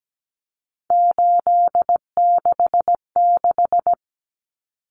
Morse code, heard as a keyed tone, about 17 wpm, 700 Hz. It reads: 866
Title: Common Cut Numbers - 3-Digit 17wpm